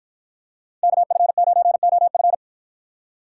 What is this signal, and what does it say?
Morse code, heard as a keyed tone, about 53 wpm, 700 Hz. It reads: KF9OF